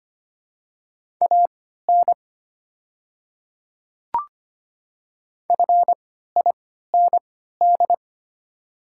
Morse code, heard as a keyed tone, about 25 wpm, spaced as for 14 wpm, 700 Hz. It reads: AN FIND
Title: 3x Character Spacing - Top 300 Words - 25wpm